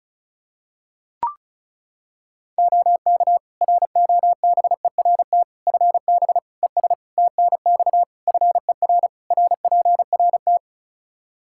Morse code, heard as a keyed tone, about 34 wpm, 700 Hz. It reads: OK ROBERT FB ES TNX FER RPRT
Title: K1USN SST QSO Practice 35wpm